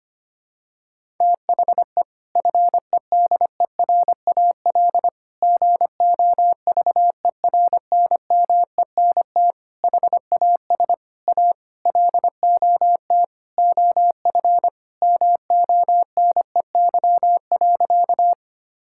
Morse code, heard as a keyed tone, about 25 wpm, 700 Hz. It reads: THE FEDERAL GOVERNMENT HAS A LOT OF MONEY.
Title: Sentences from Top 1000 Words - Review Part 5 - 25wpm